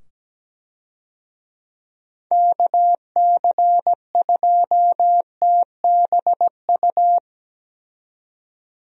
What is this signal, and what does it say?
Morse code, heard as a keyed tone, about 17 wpm, 700 Hz. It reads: KC2TBU
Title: USA Call Signs - 6 Characters - 17wpm